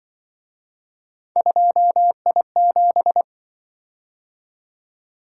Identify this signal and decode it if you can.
Morse code, heard as a keyed tone, about 25 wpm, 700 Hz. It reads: 2I7
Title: Speed-Racing - Three Letter-Number Combinations - 20wpm